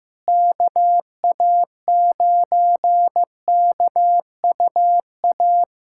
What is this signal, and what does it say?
Morse code, heard as a keyed tone, about 15 wpm, 700 Hz. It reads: KA9KUA
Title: USA Callsign DE Callsign 15wpm